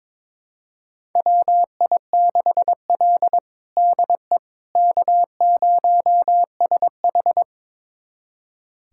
Morse code, heard as a keyed tone, about 22 wpm, 700 Hz. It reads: WI6L DE K0SH
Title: USA Callsign DE Callsign - 4 Characters - 22wpm